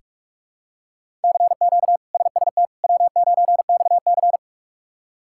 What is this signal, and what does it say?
Morse code, heard as a keyed tone, about 46 wpm, 700 Hz. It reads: CQ SST W9XC
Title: K1USN SST QSO Practice - 45wpm